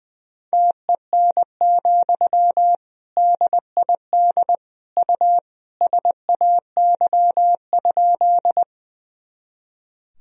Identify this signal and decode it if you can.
Morse code, heard as a keyed tone, about 20 wpm, 700 Hz. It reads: TEN, DID U SAY?
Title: Sentences from Top 400 Words and Abbreviations 20wpm